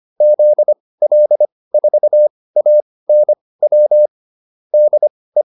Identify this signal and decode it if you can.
Morse code, heard as a keyed tone, about 25 wpm, spaced as for 18 wpm, 600 Hz. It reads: ZL4ANW DE